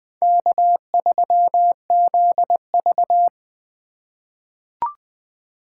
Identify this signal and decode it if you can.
Morse code, heard as a keyed tone, about 20 wpm, 700 Hz. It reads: K3ZV
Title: USA Call Signs - 4 Characters - 2x Character Spacing - 20wpm